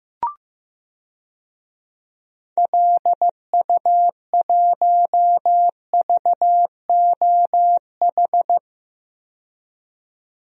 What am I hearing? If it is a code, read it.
Morse code, heard as a keyed tone, about 15 wpm, 700 Hz. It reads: LU1VOH
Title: DXCC-100 Call Signs - 6 Characters - 15wpm